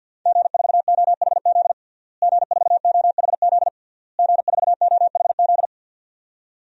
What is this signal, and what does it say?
Morse code, heard as a keyed tone, about 52 wpm, 700 Hz. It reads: G4OHZ G4OHZ G4OHZ
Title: DXCC-100 Call Signs - 5 Characters - Repeated 3x - 50wpm